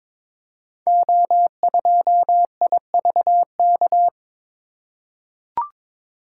Morse code, heard as a keyed tone, about 22 wpm, 700 Hz. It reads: O2IVK
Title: Code Groups - 5 Letter:Numbers - 22wpm